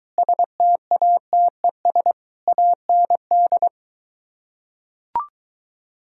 Morse code, heard as a keyed tone, about 23 wpm, 700 Hz. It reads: STATES AND